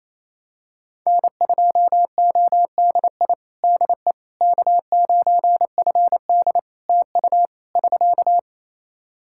Morse code, heard as a keyed tone, about 27 wpm, 700 Hz. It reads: N2ODI DE K9FD TU <SK>
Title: Practice QSO - Exchange 4 - Ending - 28wpm